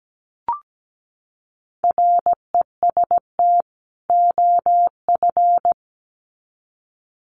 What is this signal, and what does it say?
Morse code, heard as a keyed tone, about 17 wpm, 700 Hz. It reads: REST OF
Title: Sets of 2 Words 17wpm